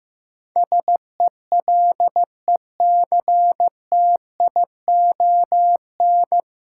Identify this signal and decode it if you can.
Morse code, heard as a keyed tone, about 15 wpm, 700 Hz. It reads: SELECTION
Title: Sentences from Top 1400 Words - New Word Drill Alt - 15wpm